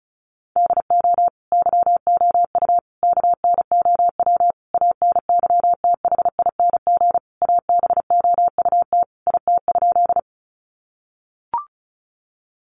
Morse code, heard as a keyed tone, about 34 wpm, 700 Hz. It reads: DO YOU KNOW ANYTHING ABOUT IT?
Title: Sentences from Top 1000 Words 35wpm